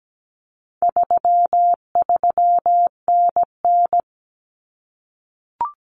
Morse code, heard as a keyed tone, about 17 wpm, 700 Hz. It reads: 33NN
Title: Common Cut Numbers - 4-Digit 17wpm